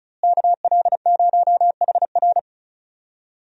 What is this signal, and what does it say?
Morse code, heard as a keyed tone, about 36 wpm, 700 Hz. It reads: KL0HR
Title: USA Call Signs - 5 Characters - Repeated 3x - 35wpm